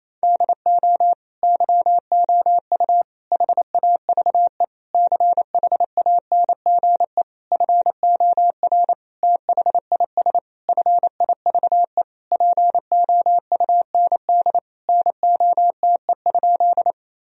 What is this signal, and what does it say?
Morse code, heard as a keyed tone, about 28 wpm, 700 Hz. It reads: DO YOU HAVE CHANGE FOR THIS FIVE POUND NOTE?
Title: Sentences from Top 1000 Words - Review Part 2 - 28wpm